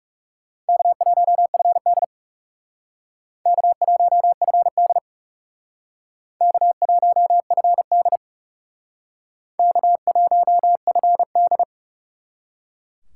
Morse code, heard as a keyed tone, about 36 wpm, 700 Hz. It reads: K1FD K1FD K1FD K1FD
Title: Speed Racing - USA 1x2 Call Signs 30wpm